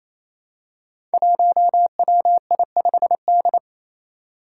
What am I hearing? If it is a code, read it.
Morse code, heard as a keyed tone, about 28 wpm, 700 Hz. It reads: 1WI5D
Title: Five Challenging Letter-Number Combinations 28wpm